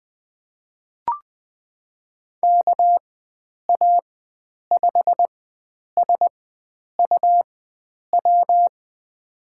Morse code, heard as a keyed tone, about 20 wpm, spaced as for 9 wpm, 700 Hz. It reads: KA5SUW
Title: USA Call Signs - 6 Characters - 4x Character Spacing - 20wpm